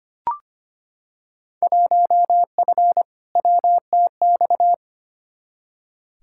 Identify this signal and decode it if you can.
Morse code, heard as a keyed tone, about 25 wpm, 700 Hz. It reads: 1F WTX
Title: Field Day Exchanges Encore - 25wpm